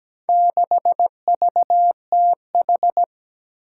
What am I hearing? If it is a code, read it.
Morse code, heard as a keyed tone, about 17 wpm, 700 Hz. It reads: 6VTH
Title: Four Challenging Letter-Number Combinations 17wpm